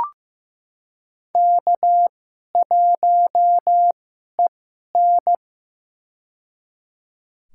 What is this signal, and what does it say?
Morse code, heard as a keyed tone, about 16 wpm, spaced as for 11 wpm, 700 Hz. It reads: K1EN
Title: USA Call Signs - 2x Character Spacing - 15wpm